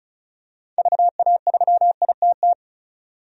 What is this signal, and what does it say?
Morse code, heard as a keyed tone, about 36 wpm, 700 Hz. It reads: VA3ITT
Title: DXCC-100 Call Signs - 6 Characters - 35wpm